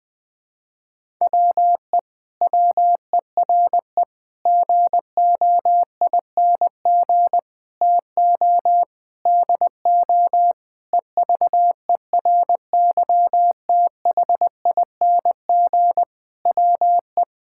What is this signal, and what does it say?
Morse code, heard as a keyed tone, about 20 wpm, 700 Hz. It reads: WE WERE GOING TO DO EVERYTHING WE
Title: Sentences with 8-Word Phrases - Spoken Beginning - 20wpm